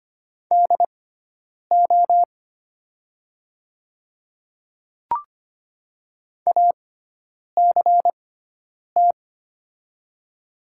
Morse code, heard as a keyed tone, about 25 wpm, spaced as for 9 wpm, 700 Hz. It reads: DO ACT
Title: 6x Character Spacing - Top 500 Words - 25wpm